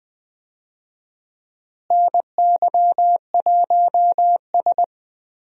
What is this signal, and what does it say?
Morse code, heard as a keyed tone, about 20 wpm, 700 Hz. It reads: NY1S
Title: USA Call Signs - 4 Characters - Repeated 3x - 20wpm